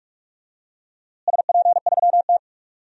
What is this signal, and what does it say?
Morse code, heard as a keyed tone, about 46 wpm, 700 Hz. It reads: SP3T
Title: DXCC-100 Call Signs - 4 Characters - Repeated 3x - 45wpm